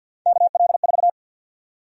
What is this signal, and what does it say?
Morse code, heard as a keyed tone, about 53 wpm, 700 Hz. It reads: KL4
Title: USA Call Sign Prefixes - 50wpm